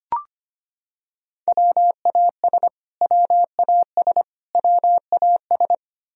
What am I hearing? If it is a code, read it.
Morse code, heard as a keyed tone, about 25 wpm, 700 Hz. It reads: WAS WAS WAS
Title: Top 100 Words Repeated 25wpm